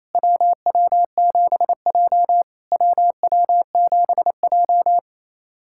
Morse code, heard as a keyed tone, about 28 wpm, 690 Hz. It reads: WW7J WW7J
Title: SOTA QSO Practice 28wpm